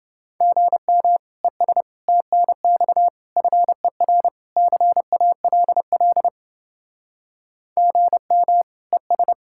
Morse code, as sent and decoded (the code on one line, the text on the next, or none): --. -- / . ... / - -. -..- / ..-. .-. / -.-. .- .-.. .-.. / --. -- / . ...
GM ES TNX FER CALL GM ES